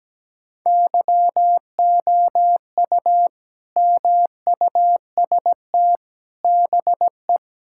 Morse code, heard as a keyed tone, about 17 wpm, 700 Hz. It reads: YOU MUST BE
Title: Sentences from Top 1600 Words 17wpm